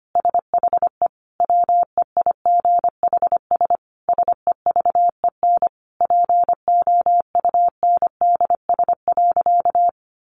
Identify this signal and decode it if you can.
Morse code, heard as a keyed tone, about 25 wpm, 700 Hz. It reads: SHE WEIGHS SEVEN POUNDS.
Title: Easy Phrases 25wpm